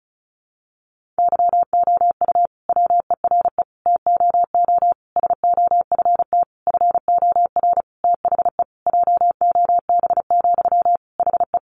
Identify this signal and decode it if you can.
Morse code, heard as a keyed tone, about 34 wpm, 700 Hz. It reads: YOU WERE TOO SOFT FOR THE JOB, HE
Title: Sentences from Top 1500 Words 35wpm